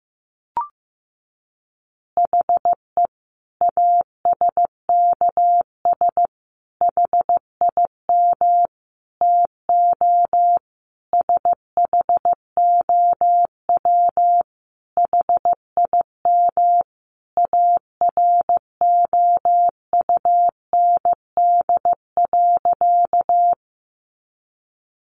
Morse code, heard as a keyed tone, about 15 wpm, 700 Hz. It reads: HE ASKS HIM TO SHOW HIM AROUND.
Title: Easy Phrases 15wpm